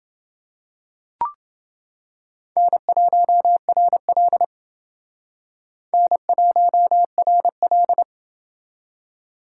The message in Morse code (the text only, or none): N1RL N1RL